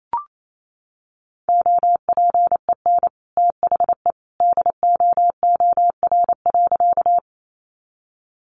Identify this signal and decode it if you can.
Morse code, heard as a keyed tone, about 28 wpm, 700 Hz. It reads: OPEN THE DOOR.
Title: Sentences from Top 300 Words 28wpm